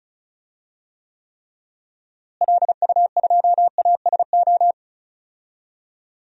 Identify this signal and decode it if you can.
Morse code, heard as a keyed tone, about 34 wpm, 700 Hz. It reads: LU2ASO
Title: DXCC-100 Call Signs - 6 Characters - Repeated 3x - 35wpm